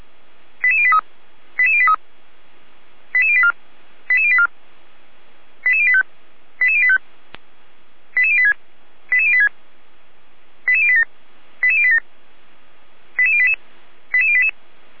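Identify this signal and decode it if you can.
Five-tone selective call (ZVEI), heard as quick runs of five tones, tones of about 80 ms. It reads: 88083 88083 88084 88084 88085 88085 88086 88086 88087 88087 88088 88088
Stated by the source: ZVEI five-tone call received on an SDR.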